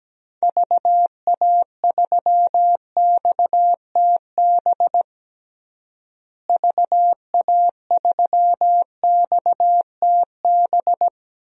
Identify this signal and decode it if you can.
Morse code, heard as a keyed tone, about 17 wpm, 700 Hz. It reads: VA3XTB VA3XTB
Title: DXCC-100 Call Signs - 6 Characters - Repeated 3x - 17wpm